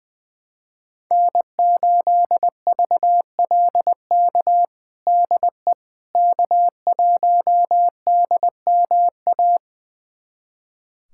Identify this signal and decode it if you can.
Morse code, heard as a keyed tone, about 20 wpm, 700 Hz. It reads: N8VLK DE K1DMA
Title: USA Callsign DE Callsign - 5 Characters - 20wpm